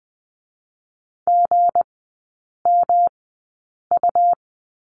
Morse code, heard as a keyed tone, about 20 wpm, spaced as for 8 wpm, 690 Hz. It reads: GMU
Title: 159 Lesson 26e - Letter g - Three Characters 20wpm